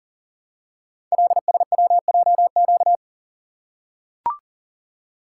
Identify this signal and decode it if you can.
Morse code, heard as a keyed tone, about 42 wpm, 700 Hz. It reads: LSWJQ
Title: Code Groups - 5 Letter:Numbers - 40wpm